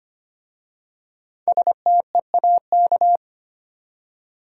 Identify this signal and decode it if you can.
Morse code, heard as a keyed tone, about 25 wpm, 700 Hz. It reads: STEAK
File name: practice / 25wpm / Easy Words with Hints - 25wpm